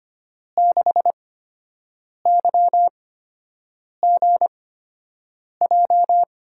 Morse code, heard as a keyed tone, about 25 wpm, spaced as for 7 wpm, 700 Hz. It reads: 6YGJ
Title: USA Call Signs - 8x Character Spacing - 25wpm